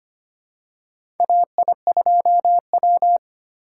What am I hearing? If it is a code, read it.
Morse code, heard as a keyed tone, about 25 wpm, 700 Hz. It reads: AI2W
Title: USA Call Signs - 4 Characters - 4x Character Spacing - 25wpm